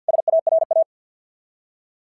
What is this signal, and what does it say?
Morse code, heard as a keyed tone, about 50 wpm, 690 Hz. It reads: SARA
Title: SOTA QSO Practice 50wpm